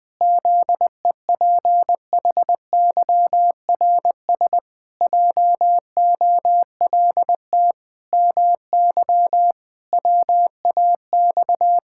Morse code, heard as a keyed tone, about 20 wpm, 700 Hz. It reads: ZEPHYRS JOLT MY WAX
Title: Pangram Challenge v2 - 20wpm